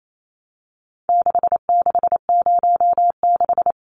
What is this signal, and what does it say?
Morse code, heard as a keyed tone, about 28 wpm, 700 Hz. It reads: 6606
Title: Numbers - 4-Digit 28wpm